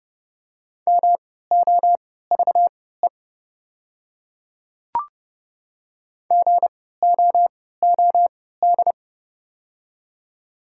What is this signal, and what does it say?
Morse code, heard as a keyed tone, about 29 wpm, spaced as for 17 wpm, 690 Hz. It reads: MOVE GOOD
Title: 3x Character Spacing - Top 200 Words - 30wpm